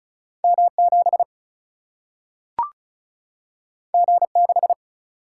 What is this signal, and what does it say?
Morse code, heard as a keyed tone, about 36 wpm, 700 Hz. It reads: M 7 G 6
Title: Single-Letter Single-Number 35wpm